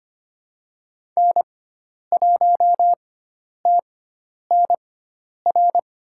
Morse code, heard as a keyed tone, about 25 wpm, spaced as for 10 wpm, 700 Hz. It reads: N1TNR
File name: course / 25:10wpm / 095 Lesson 16g - Letter c - Call Signs 25wpm